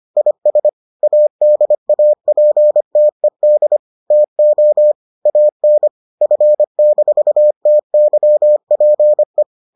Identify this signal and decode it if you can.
Morse code, heard as a keyed tone, about 25 wpm, 600 Hz. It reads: IS ADAPTED TO AN F-TYPE